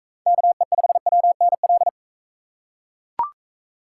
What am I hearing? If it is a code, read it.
Morse code, heard as a keyed tone, about 42 wpm, 700 Hz. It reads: KE5WNL